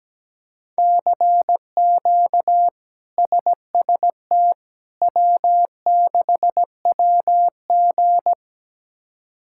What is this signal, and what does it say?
Morse code, heard as a keyed tone, about 17 wpm, 700 Hz. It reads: CQ SST W6WG
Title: K1USN SST QSO Practice - 17wpm